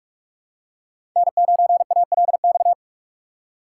Morse code, heard as a keyed tone, about 44 wpm, 700 Hz. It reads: N9ALX
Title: USA Call Signs - 5 Characters - 6x Character Spacing - 45wpm